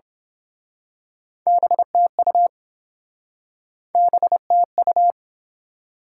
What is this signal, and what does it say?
Morse code, heard as a keyed tone, about 28 wpm, 690 Hz. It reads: BTU BTU